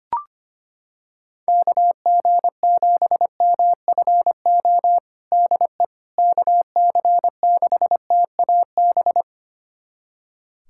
Morse code, heard as a keyed tone, about 25 wpm, 700 Hz. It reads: KG7MFO DE KC6TAB
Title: USA Callsign DE Callsign 25wpm